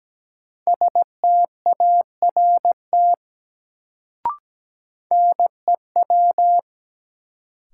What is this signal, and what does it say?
Morse code, heard as a keyed tone, about 17 wpm, 700 Hz. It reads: START NEW
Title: Sentences from Top 300 Words - New Word Drill - 17wpm